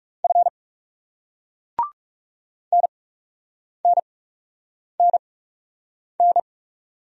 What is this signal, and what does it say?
Morse code, heard as a keyed tone, about 39 wpm, spaced as for 8 wpm, 700 Hz. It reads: F NNNN